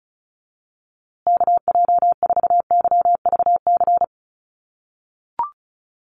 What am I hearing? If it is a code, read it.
Morse code, heard as a keyed tone, about 34 wpm, 700 Hz. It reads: KJ4YVC
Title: USA Call Signs Repeated 2x 35wpm